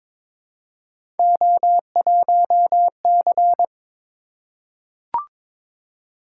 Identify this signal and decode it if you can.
Morse code, heard as a keyed tone, about 22 wpm, 700 Hz. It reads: O1C
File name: practice / 22wpm / Three Letter-Number Combinations - 22wpm